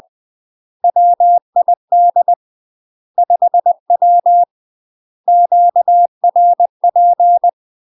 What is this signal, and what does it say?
Morse code, heard as a keyed tone, about 20 wpm, 700 Hz. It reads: WID 5W QRP
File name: UK MASTER QSOs TRAINING/20WPM